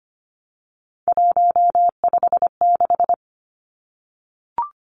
Morse code, heard as a keyed tone, about 25 wpm, 700 Hz. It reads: 156
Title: Numbers - 3-Digit 25wpm